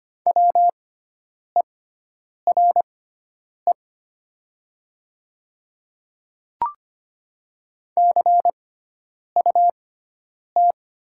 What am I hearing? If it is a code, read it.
Morse code, heard as a keyed tone, about 25 wpm, spaced as for 9 wpm, 700 Hz. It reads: WERE CUT
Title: 6x Character Spacing - Top 300 Words - 25wpm